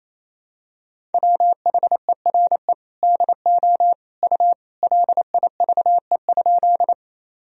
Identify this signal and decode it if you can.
Morse code, heard as a keyed tone, about 28 wpm, 700 Hz. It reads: WHERE DO U LIVE?
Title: Sentences from Top 200 Words and Abbreviations 28wpm